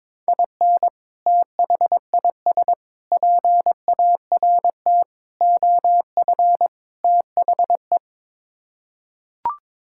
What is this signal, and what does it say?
Morse code, heard as a keyed tone, about 22 wpm, 700 Hz. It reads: IN THIS PART OF THE
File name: practice / 22wpm / Build A Phrase - 5-Word Phrases - 22wpm